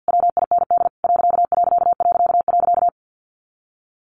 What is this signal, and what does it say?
Morse code, heard as a keyed tone, about 50 wpm, 700 Hz. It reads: WIND ....
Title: A Sound Of Thunder 50wpm